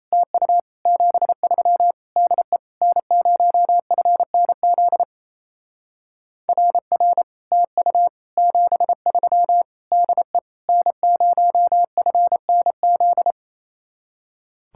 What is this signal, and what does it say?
Morse code, heard as a keyed tone, about 36 wpm, 700 Hz. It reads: TU 73 DE N0FNZ RR TU 73 DE N0FNZ